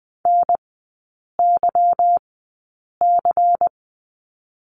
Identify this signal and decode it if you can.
Morse code, heard as a keyed tone, about 20 wpm, spaced as for 8 wpm, 690 Hz. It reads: NYC
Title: 145 Lesson 24e - Letter y - Three Characters 20wpm